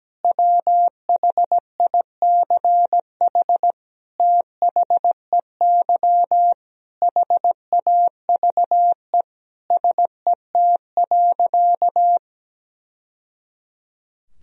Morse code, heard as a keyed tone, about 17 wpm, 700 Hz. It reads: WHICH THEY HAVE SET.
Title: Sentences from Top 1000 Words - Review Part 2 - 17wpm